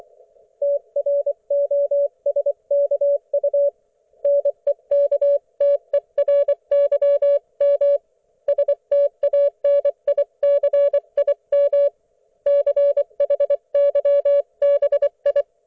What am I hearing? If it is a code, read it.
Morse code, heard as a keyed tone, about 24 wpm, 570 Hz. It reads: TROSKU NEKTERYM STANICIM CHYBI
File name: Archiv / CUC224 / ok1fhi cul224